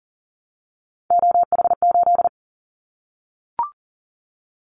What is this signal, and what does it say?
Morse code, heard as a keyed tone, about 38 wpm, 700 Hz. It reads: OH8
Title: Callsign Prefixes for Top 50 DXCC - 40wpm